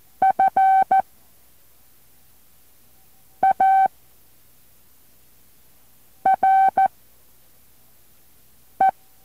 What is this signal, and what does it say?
Morse code, heard as a keyed tone, about 14 wpm, 770 Hz. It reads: F A R E
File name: intl morse code2